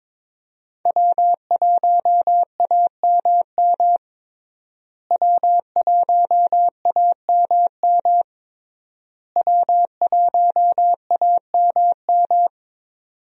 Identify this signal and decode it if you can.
Morse code, heard as a keyed tone, about 22 wpm, 700 Hz. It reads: W1AMM W1AMM W1AMM
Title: USA Call Signs - 5 Characters - Repeated 3x - 22wpm